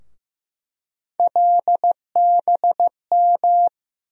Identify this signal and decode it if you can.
Morse code, heard as a keyed tone, about 15 wpm, 700 Hz. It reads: LBM
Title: USA Call Sign Postfixes - 3 Letters 15wpm